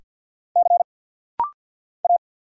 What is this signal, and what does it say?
Morse code, heard as a keyed tone, about 50 wpm, 700 Hz. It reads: C A
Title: Single Letters - Rapid-Fire 50wpm